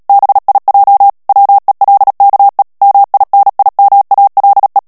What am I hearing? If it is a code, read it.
Morse code, heard as a keyed tone, about 37 wpm, 800 Hz. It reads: BIJ WELKE MINIMALE
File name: morse mp3 / HOLLANDAIS texte clair 2 vitesse 35 mots minute